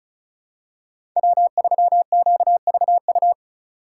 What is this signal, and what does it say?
Morse code, heard as a keyed tone, about 36 wpm, 700 Hz. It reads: W3QVU
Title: USA Call Signs - 5 Characters - 2x Character Spacing - 35wpm